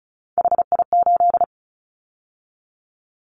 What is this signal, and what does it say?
Morse code, heard as a keyed tone, about 34 wpm, 700 Hz. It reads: HI8
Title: Callsign Prefixes for Top 100 DXCC - 35wpm